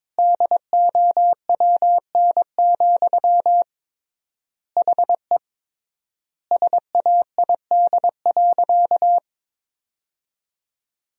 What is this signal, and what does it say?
Morse code, heard as a keyed tone, about 21 wpm, 700 Hz. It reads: DOWN, HE SAID.
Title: 3x Word-Spacing - Sentences from Top 500 Words - 22wpm